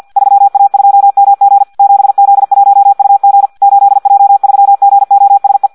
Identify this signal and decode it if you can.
Morse code, heard as a keyed tone, about 50 wpm, 800 Hz. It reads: 3A2MG 7Z1UG 8J3GOSE
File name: Audio practica CW / qrq-tx-50 2